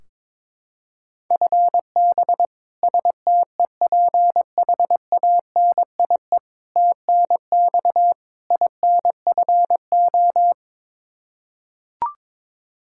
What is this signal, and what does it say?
Morse code, heard as a keyed tone, about 22 wpm, 700 Hz. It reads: FB STEPHANIE TNX INFO
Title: Practice QSO - Exchange 3 - More Info - 22wpm